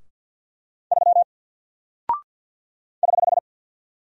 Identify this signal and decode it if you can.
Morse code, heard as a keyed tone, about 49 wpm, 700 Hz. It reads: 3 <HH>
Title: Single Character (Letter:Number:Punctuation:Prosign) 50wpm